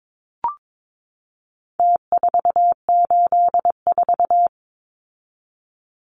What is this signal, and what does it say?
Morse code, heard as a keyed tone, about 22 wpm, 700 Hz. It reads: T484
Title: Numbers - Contest Style - 4-Digit 22wpm